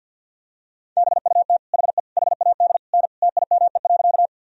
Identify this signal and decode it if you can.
Morse code, heard as a keyed tone, about 48 wpm, 700 Hz. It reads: BUT HE HAD N TIME.
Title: Sentences from Top 100 Words and Abbreviations 50wpm